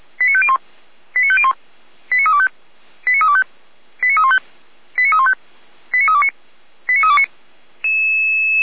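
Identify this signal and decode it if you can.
Five-tone selective call (ZVEI), heard as quick runs of five tones, tones of about 70 ms. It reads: 89501 89501 89325 89325 89315 89315 89329 89329
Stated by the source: ZVEI five-tone selective call received on SDR.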